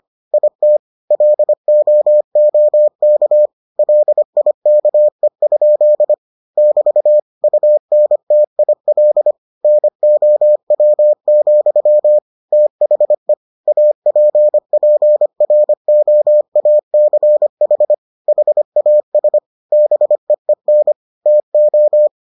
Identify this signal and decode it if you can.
Morse code, heard as a keyed tone, about 25 wpm, 600 Hz. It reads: IT LOOK LIKE? <BT> UNTIL NOW, THE APPROACH HAS BEEN TO